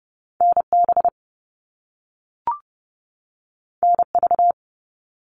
Morse code, heard as a keyed tone, about 30 wpm, 700 Hz. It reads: NB NV
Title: All Cut Numbers - 2-Digit 30wpm